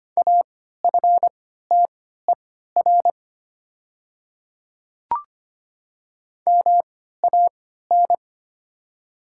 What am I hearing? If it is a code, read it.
Morse code, heard as a keyed tone, about 25 wpm, spaced as for 14 wpm, 690 Hz. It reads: AFTER MAN
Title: 3x Character Spacing - Top 300 Words - 25wpm